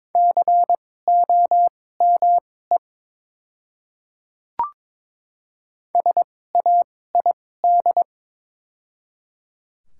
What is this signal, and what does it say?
Morse code, heard as a keyed tone, about 22 wpm, spaced as for 16 wpm, 700 Hz. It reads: COME SAID
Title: 2x Character Spacing - Top 200 Words - 22wpm